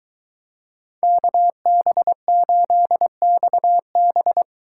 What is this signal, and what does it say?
Morse code, heard as a keyed tone, about 22 wpm, 700 Hz. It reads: KB8XB